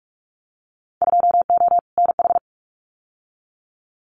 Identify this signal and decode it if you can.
Morse code, heard as a keyed tone, about 46 wpm, 690 Hz. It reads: 2O NH
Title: Winter Field Day Exchanges Encore 45wpm